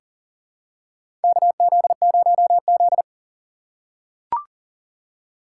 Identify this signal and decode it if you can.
Morse code, heard as a keyed tone, about 38 wpm, 690 Hz. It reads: KZ0Z
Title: USA Call Signs - 4 Characters - 2x Character Spacing - 40wpm